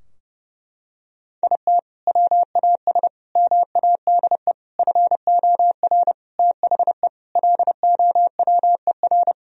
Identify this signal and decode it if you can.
Morse code, heard as a keyed tone, about 30 wpm, 700 Hz. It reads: IT WAS MADE FOR THE LOWER
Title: Sentences from Top 1000 Words - Review Part 4 - 30wpm